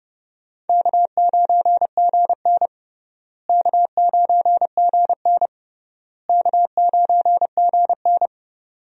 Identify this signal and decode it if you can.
Morse code, heard as a keyed tone, about 31 wpm, 700 Hz. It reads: K9GN K9GN K9GN
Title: USA Call Signs - 4 Characters - Repeated 3x - 30wpm